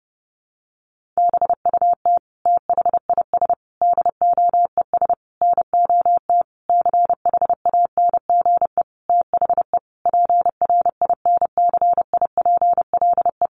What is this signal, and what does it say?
Morse code, heard as a keyed tone, about 30 wpm, 700 Hz. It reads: BUT THIS DOES NOT CHANGE THE PRINCIPLE
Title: Sentences from Top 1400 Words 30wpm